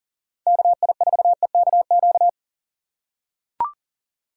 Morse code, heard as a keyed tone, about 40 wpm, 700 Hz. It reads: KI4EKQ
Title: USA Call Signs - 6 Characters - 3x Character Spacing - 40wpm